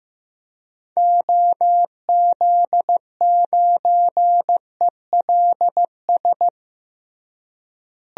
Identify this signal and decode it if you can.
Morse code, heard as a keyed tone, about 16 wpm, 700 Hz. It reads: OZ9ELS
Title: DXCC-100 Call Signs - 6 Characters - Repeated 2x - 15wpm